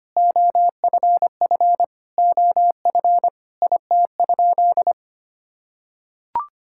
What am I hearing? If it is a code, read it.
Morse code, heard as a keyed tone, about 25 wpm, 700 Hz. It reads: OFF OF IT?
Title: Sentences from Top 1000 Words - Review Part 5 - 25wpm